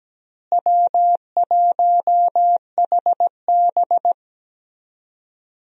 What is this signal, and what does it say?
Morse code, heard as a keyed tone, about 17 wpm, 700 Hz. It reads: W1HB
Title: USA Call Signs - 4 Characters - Repeated 3x - 17wpm